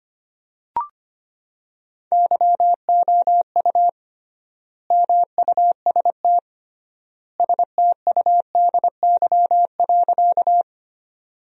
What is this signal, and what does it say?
Morse code, heard as a keyed tone, about 25 wpm, 700 Hz. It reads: YOU MUST STUDY.
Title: 3x Word-Spacing - Sentences from Top 300 Words - 25wpm